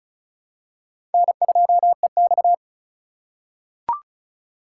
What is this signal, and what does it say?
Morse code, heard as a keyed tone, about 36 wpm, 700 Hz. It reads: N2EX
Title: USA Call Signs - 4 Characters - 3x Character Spacing - 35wpm